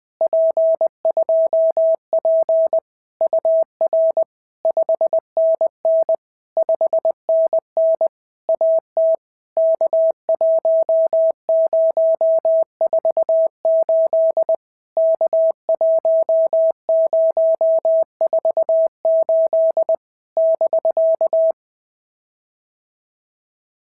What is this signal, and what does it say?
Morse code, heard as a keyed tone, about 20 wpm, 650 Hz. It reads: P2P UR 5NN 5NN AT K1048 K1048 <BK>
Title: POTA QSO Practice - 20wpm